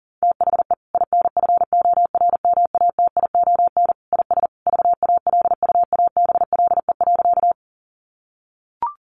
Morse code, heard as a keyed tone, about 40 wpm, 700 Hz. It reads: THE INFORMATION IS VALUABLE.
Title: Sentences from Top 1100 Words 40wpm